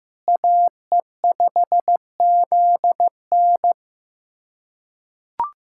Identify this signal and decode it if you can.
Morse code, heard as a keyed tone, about 15 wpm, 700 Hz. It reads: AE5ZN
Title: USA Callsign DE Callsign 15wpm